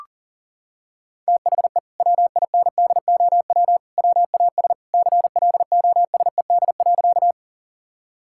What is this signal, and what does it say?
Morse code, heard as a keyed tone, about 37 wpm, 700 Hz. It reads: THE WINDOW WAS CLOSED.
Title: Sentences from Top 1600 Words 40wpm